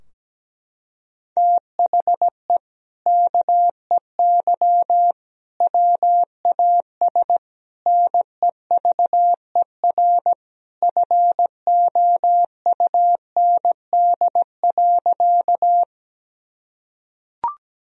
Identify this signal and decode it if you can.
Morse code, heard as a keyed tone, about 17 wpm, 700 Hz. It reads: THE KEY WAS NEVER FOUND.
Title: Sentences from Top 1000 Words - Review Part 4 - 17wpm